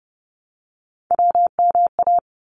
Morse code, heard as a keyed tone, about 30 wpm, 700 Hz. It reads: WMA
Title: ARRL Sections 30wpm